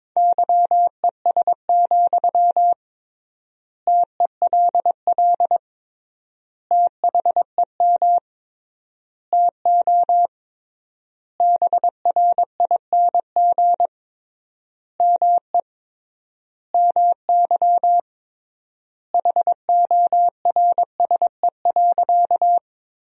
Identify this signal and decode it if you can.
Morse code, heard as a keyed tone, about 22 wpm, 700 Hz. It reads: YES, TELL THEM TO BRING ME MY HORSE.
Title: 3x Word-Spacing - Sentences from Top 500 Words - 22wpm